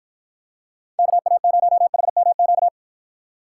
Morse code, heard as a keyed tone, about 53 wpm, 700 Hz. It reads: KA0HMQ